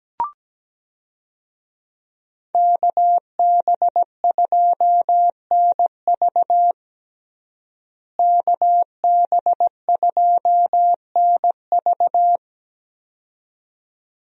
Morse code, heard as a keyed tone, about 17 wpm, 700 Hz. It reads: KB2NV KB2NV
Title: USA Call Signs - 5 Characters - Repeated 2x - 17wpm